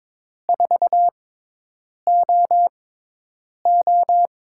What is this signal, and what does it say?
Morse code, heard as a keyed tone, about 22 wpm, spaced as for 8 wpm, 700 Hz. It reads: 4OO
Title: USA Call Signs - 6x Character Spacing - 22wpm